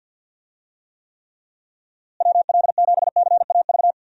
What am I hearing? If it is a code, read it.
Morse code, heard as a keyed tone, about 52 wpm, 700 Hz. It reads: WL7CAV
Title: USA Call Signs - 6 Characters - Repeated 3x - 50wpm